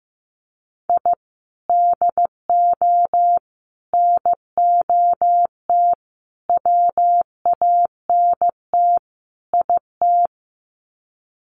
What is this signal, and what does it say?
Morse code, heard as a keyed tone, about 15 wpm, 700 Hz. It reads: I DO NOT WANT IT
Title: Sets of 5 Words Encore2 15wpm